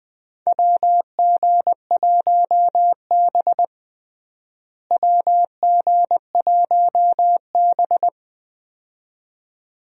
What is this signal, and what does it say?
Morse code, heard as a keyed tone, about 20 wpm, 700 Hz. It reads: WG1B WG1B
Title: USA Call Signs - 4 Characters - Repeated 2x - 20wpm